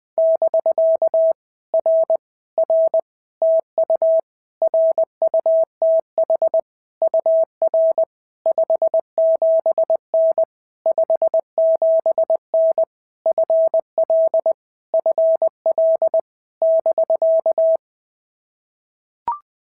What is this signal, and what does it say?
Morse code, heard as a keyed tone, about 19 wpm, 650 Hz. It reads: <BK> R R TU RUTH UR 57N 57N FL FL <BK>
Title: POTA QSO Practice - 20wpm